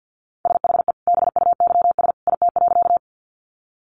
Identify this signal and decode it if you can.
Morse code, heard as a keyed tone, about 50 wpm, 700 Hz. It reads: SHE BUYS IT.
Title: Additional Easy Phrases 50wpm